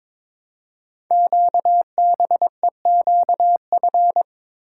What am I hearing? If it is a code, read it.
Morse code, heard as a keyed tone, about 22 wpm, 700 Hz. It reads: QBEQF
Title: Code Groups - 5 Letters - 22wpm